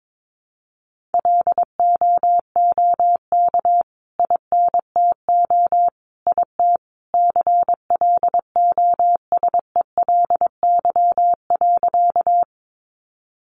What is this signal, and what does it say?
Morse code, heard as a keyed tone, about 22 wpm, 700 Hz. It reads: LOOK INTO IT CLOSELY.
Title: Sentences from Top 1400 Words 22wpm